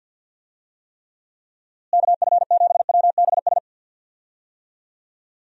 Morse code, heard as a keyed tone, about 51 wpm, 700 Hz. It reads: KF7WBS
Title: USA Call Signs - 6 Characters - Repeated 3x - 50wpm